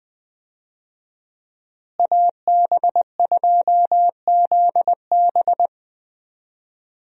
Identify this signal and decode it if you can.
Morse code, heard as a keyed tone, about 20 wpm, 700 Hz. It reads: AB2ZB
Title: USA Call Signs - 5 Characters - Repeated 3x - 20wpm